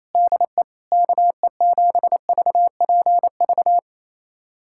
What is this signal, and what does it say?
Morse code, heard as a keyed tone, about 27 wpm, 700 Hz. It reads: DE KE7VPV
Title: USA Callsign DE Callsign - 6 Characters - 28wpm